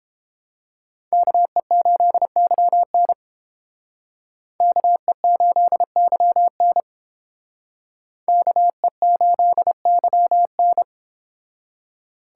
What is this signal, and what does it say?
Morse code, heard as a keyed tone, about 29 wpm, 700 Hz. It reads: KE8YN KE8YN KE8YN